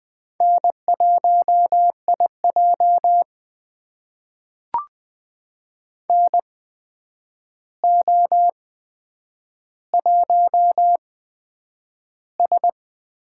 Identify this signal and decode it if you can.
Morse code, heard as a keyed tone, about 20 wpm, spaced as for 9 wpm, 700 Hz. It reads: N1IJ NO1S
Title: USA Call Signs - 4 Characters - 8x Character Spacing - 20wpm